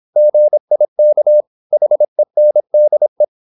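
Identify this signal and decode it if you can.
Morse code, heard as a keyed tone, about 26 wpm, 600 Hz. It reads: GIK HENDE
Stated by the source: High-speed (QRQ) Morse at 26 wpm.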